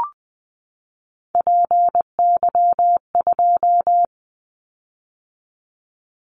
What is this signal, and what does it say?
Morse code, heard as a keyed tone, about 20 wpm, 700 Hz. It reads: PY2
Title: Callsign Prefixes for Top 100 DXCC - 20wpm